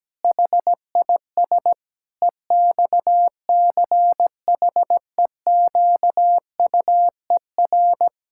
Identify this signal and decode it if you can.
Morse code, heard as a keyed tone, about 17 wpm, 700 Hz. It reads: HIS EXCHEQUER
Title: Pangram Challenge v2 - 17wpm